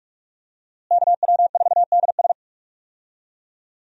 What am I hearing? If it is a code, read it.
Morse code, heard as a keyed tone, about 43 wpm, 700 Hz. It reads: KW4DS
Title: USA Call Signs - 5 Characters - Repeated 2x - 45wpm